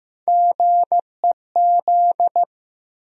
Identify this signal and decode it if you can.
Morse code, heard as a keyed tone, about 15 wpm, 700 Hz. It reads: GEZ
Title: USA Call Sign Postfixes - 3 Letters 15wpm